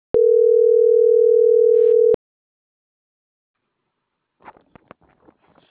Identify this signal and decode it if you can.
Ringback tone (North America), 440 + 480 Hz.